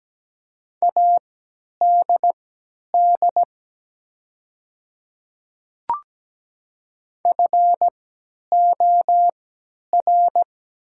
Morse code, heard as a keyed tone, about 17 wpm, spaced as for 10 wpm, 700 Hz. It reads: ADD FOR
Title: 3x Character Spacing - Top 300 Words - 17wpm